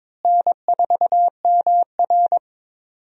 Morse code, heard as a keyed tone, about 22 wpm, 700 Hz. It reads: N4MR
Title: USA Call Signs - 4 Characters - Repeated 3x - 22wpm